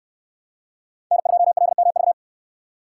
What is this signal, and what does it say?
Morse code, heard as a keyed tone, about 69 wpm, 700 Hz. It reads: N2LMV